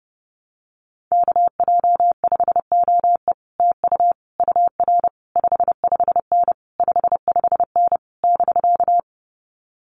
Morse code, heard as a keyed tone, about 30 wpm, 700 Hz. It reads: KJ5OE TU UR 55N 55N <BK>
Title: Common QSO Phrases Encore 30wpm